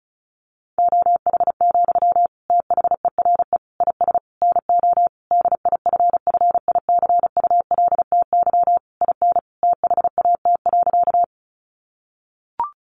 Morse code, heard as a keyed tone, about 34 wpm, 700 Hz. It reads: OH, THERE IS NO DIFFICULTY IN THAT.
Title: Sentences from Top 1500 Words 35wpm